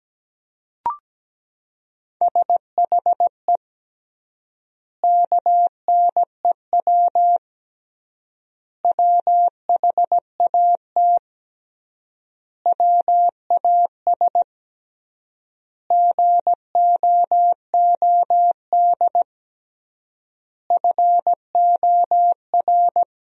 Morse code, heard as a keyed tone, about 17 wpm, 700 Hz. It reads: SHE KNEW WHAT WAS GOOD FOR
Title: 3x Word-Spacing - Sentences from Top 400 Words - 17wpm